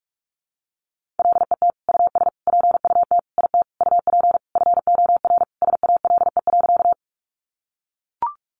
Morse code, heard as a keyed tone, about 44 wpm, 700 Hz. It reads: LET US PUT IT UP FOR SALE.
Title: Sentences from Top 1400 Words 45wpm